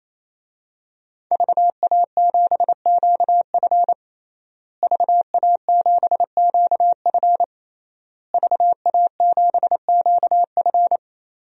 Morse code, heard as a keyed tone, about 28 wpm, 700 Hz. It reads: VA7QF VA7QF VA7QF
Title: DXCC-100 Call Signs - 5 Characters - Repeated 3x - 28wpm